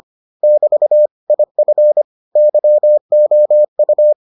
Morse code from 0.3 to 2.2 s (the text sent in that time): <BT> IF